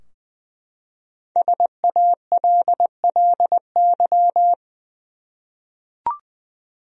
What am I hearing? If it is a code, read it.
Morse code, heard as a keyed tone, about 20 wpm, 700 Hz. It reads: SALLY
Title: USA Common Names - 20wpm